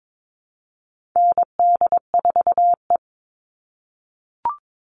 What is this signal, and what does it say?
Morse code, heard as a keyed tone, about 22 wpm, 700 Hz. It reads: ND4E
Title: All Cut Numbers - 4-Digit 22wpm